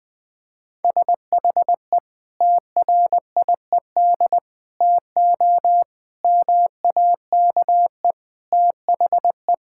Morse code, heard as a keyed tone, about 20 wpm, 700 Hz. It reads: SHE TRIED TO MAKE THE
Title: Sentences with 3-Word Phrases - Spoken Beginning - 20wpm